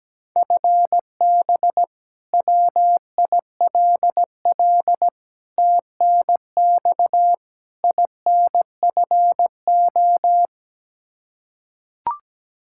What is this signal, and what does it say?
Morse code, heard as a keyed tone, about 17 wpm, 700 Hz. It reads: FB WILL TNX INFO
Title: Practice QSO - Exchange 3 - More Info - 17wpm